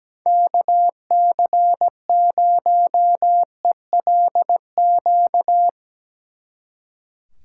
Morse code, heard as a keyed tone, about 20 wpm, 700 Hz. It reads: KC0ELQ
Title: Speed Racing - USA 2x3 Call Signs 17wpm